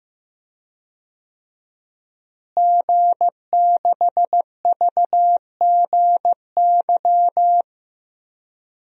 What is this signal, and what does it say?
Morse code, heard as a keyed tone, about 15 wpm, 700 Hz. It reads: G6VGY
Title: DXCC-100 Call Signs - 5 Characters - Repeated 2x - 15wpm